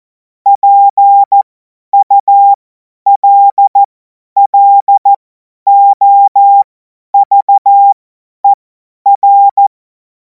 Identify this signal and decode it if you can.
Morse code, heard as a keyed tone, about 14 wpm, spaced as for 10 wpm, 800 Hz. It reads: PULLOVER
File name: Morsekurs / Witze / Witz 03. März 2026 50 F70